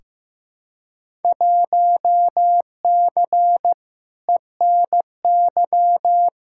Morse code, heard as a keyed tone, about 15 wpm, 700 Hz. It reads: 1C ENY
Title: Field Day Exchanges Encore - 15wpm